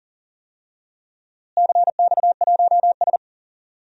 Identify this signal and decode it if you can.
Morse code, heard as a keyed tone, about 37 wpm, 700 Hz. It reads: CX1S
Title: DXCC-100 Call Signs - 4 Characters - Repeated 3x - 40wpm